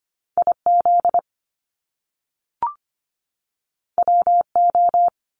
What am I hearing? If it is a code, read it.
Morse code, heard as a keyed tone, about 25 wpm, 700 Hz. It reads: IZ WO
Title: Top Two Letter Combinations in Words 25wpm